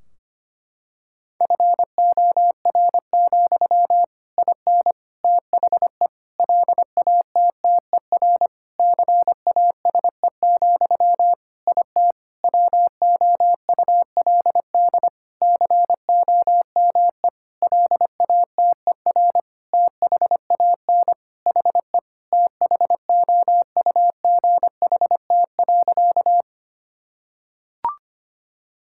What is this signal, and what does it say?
Morse code, heard as a keyed tone, about 25 wpm, 700 Hz. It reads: FOR, IN THE LATTER CASE, IT WOULD COME LATER THAN HE THOUGHT.
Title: Sentences from Top 1600 Words 25wpm